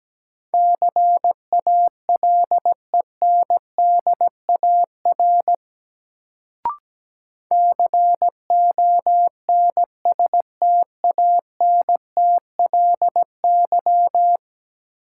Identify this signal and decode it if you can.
Morse code, heard as a keyed tone, about 17 wpm, 700 Hz. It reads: CALENDAR CONSTANTLY
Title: Sentences from Top 1600 Words - New Word Drill Alt - 17wpm